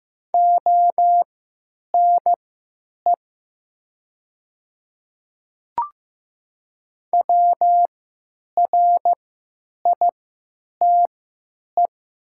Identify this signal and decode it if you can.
Morse code, heard as a keyed tone, about 15 wpm, spaced as for 9 wpm, 700 Hz. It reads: ONE WRITE
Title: 3x Character Spacing - Top 500 Words - 15wpm